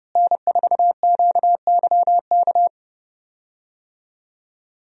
Morse code, heard as a keyed tone, about 30 wpm, 700 Hz. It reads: N4QYK
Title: USA Call Signs - 5 Characters - Repeated 2x - 30wpm